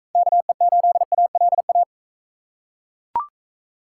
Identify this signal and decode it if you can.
Morse code, heard as a keyed tone, about 42 wpm, 700 Hz. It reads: KE8ALA